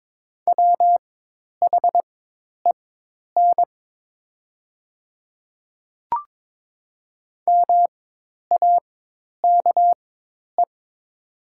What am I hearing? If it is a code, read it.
Morse code, heard as a keyed tone, about 21 wpm, spaced as for 10 wpm, 700 Hz. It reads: WHEN MAKE
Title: 4x Character Spacing - Top 500 Words - 22wpm